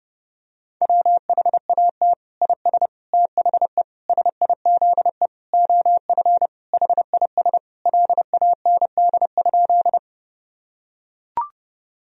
Morse code, heard as a keyed tone, about 32 wpm, 700 Hz. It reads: WHAT IS THE SIZE OF HIS LAND?
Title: Sentences from Top 1000 Words - Review Part 2 - 30wpm